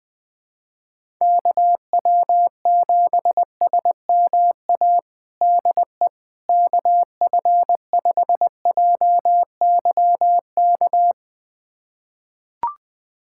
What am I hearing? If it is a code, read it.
Morse code, heard as a keyed tone, about 21 wpm, 700 Hz. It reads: KW7SMA DE KF5JYK
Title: USA Callsign DE Callsign 20wpm